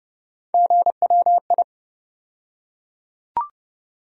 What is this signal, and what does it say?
Morse code, heard as a keyed tone, about 30 wpm, 700 Hz. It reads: GWI
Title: USA Call Sign Postfixes - 3 Letters 30wpm